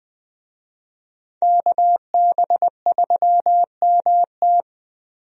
Morse code, heard as a keyed tone, about 20 wpm, 700 Hz. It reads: KB3MT
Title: USA Call Signs - 5 Characters - 20wpm